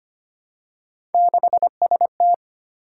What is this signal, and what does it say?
Morse code, heard as a keyed tone, about 25 wpm, 700 Hz. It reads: 6ST
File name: Three Letter-Number Combinations - 25wpm